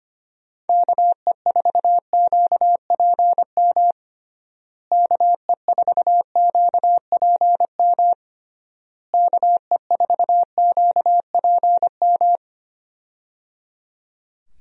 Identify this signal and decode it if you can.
Morse code, heard as a keyed tone, about 25 wpm, 700 Hz. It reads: KE4QPM KE4QPM KE4QPM
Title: USA Call Signs - 6 Characters - Repeated 3x - 25wpm